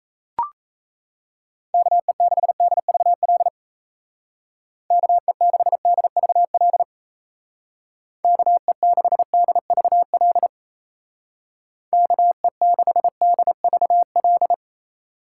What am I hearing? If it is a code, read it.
Morse code, heard as a keyed tone, about 34 wpm, 700 Hz. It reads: KE6DVL KE6DVL KE6DVL KE6DVL
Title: Speed Racing - USA 2x3 Call Signs 28wpm